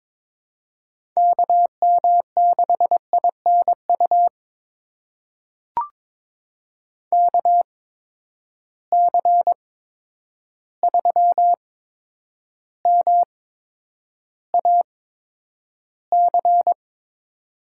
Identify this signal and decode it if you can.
Morse code, heard as a keyed tone, about 22 wpm, spaced as for 9 wpm, 700 Hz. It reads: KM6INU KC3MAC
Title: USA Call Signs - 6 Characters - 8x Character Spacing - 22wpm